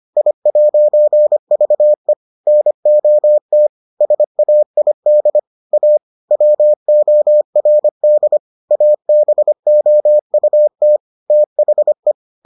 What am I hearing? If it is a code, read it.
Morse code, heard as a keyed tone, about 25 wpm, 600 Hz. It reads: I'VE NOT SAID A WORD ABOUT THE